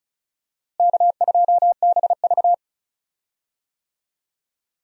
Morse code, heard as a keyed tone, about 34 wpm, 700 Hz. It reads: K2BV
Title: USA Call Signs - 4 Characters - Repeated 2x - 35wpm